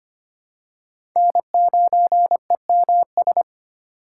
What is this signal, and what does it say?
Morse code, heard as a keyed tone, about 25 wpm, 700 Hz. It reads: N9EMS
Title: USA Call Signs - 5 Characters - 4x Character Spacing - 25wpm